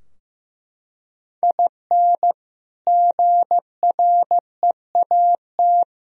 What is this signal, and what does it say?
Morse code, heard as a keyed tone, about 15 wpm, 700 Hz. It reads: IN GREAT
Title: Sentences from Top 1000 Words - Review Part 2 - 15wpm